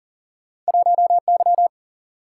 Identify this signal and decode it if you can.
Morse code, heard as a keyed tone, about 40 wpm, 700 Hz. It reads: 1 Y
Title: Single-Letter Single-Number 40wpm